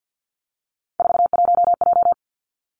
Morse code, heard as a keyed tone, about 50 wpm, 700 Hz. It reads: 41J
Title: Three Challenging Letter-Number Combinations 50wpm